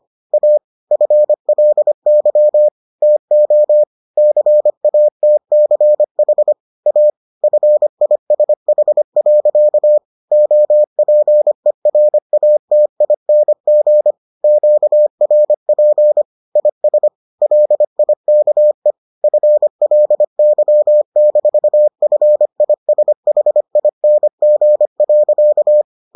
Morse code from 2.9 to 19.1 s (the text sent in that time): TO CATCH A FISH. OPERATING QRP IS LIKE